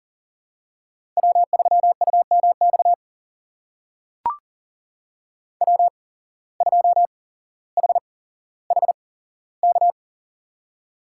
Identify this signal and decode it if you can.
Morse code, heard as a keyed tone, about 40 wpm, spaced as for 17 wpm, 700 Hz. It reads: W3UMX W2HHK
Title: USA Call Signs - 5 Characters - 8x Character Spacing - 40wpm